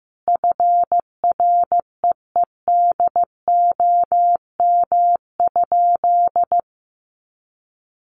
Morse code, heard as a keyed tone, about 15 wpm, 700 Hz. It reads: FREEDOM?
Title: Sentences from Top 1500 Words 15wpm